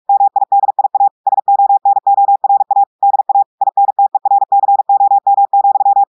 Morse code, heard as a keyed tone, about 46 wpm, 800 Hz. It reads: MIDIA SONORA DA INTERCOM,
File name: portugais texte en clair 1 vitesse 45 mots minutes